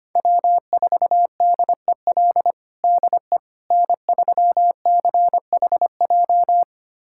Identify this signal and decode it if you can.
Morse code, heard as a keyed tone, about 25 wpm, 700 Hz. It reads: W4DEL DE N3CHJ
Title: USA Callsign DE Callsign - 5 Characters - 25wpm